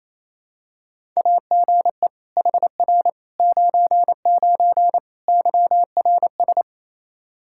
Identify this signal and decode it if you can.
Morse code, heard as a keyed tone, about 28 wpm, 700 Hz. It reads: AGE HR 99 YRS
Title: Practice QSO - Exchange 3 - More Info - 28wpm